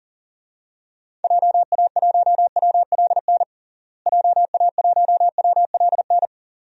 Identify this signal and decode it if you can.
Morse code, heard as a keyed tone, about 40 wpm, 700 Hz. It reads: JA1WLN JA1WLN
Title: DXCC-100 Call Signs - 6 Characters - Repeated 2x - 40wpm